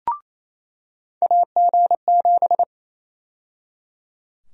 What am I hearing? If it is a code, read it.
Morse code, heard as a keyed tone, about 28 wpm, 700 Hz. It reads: AG7
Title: USA Call Sign Prefixes - 28wpm